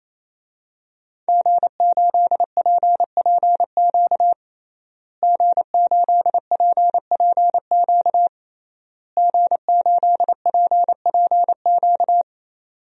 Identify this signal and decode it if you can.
Morse code, heard as a keyed tone, about 28 wpm, 700 Hz. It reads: G8PPQ G8PPQ G8PPQ
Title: DXCC-100 Call Signs - 5 Characters - Repeated 3x - 28wpm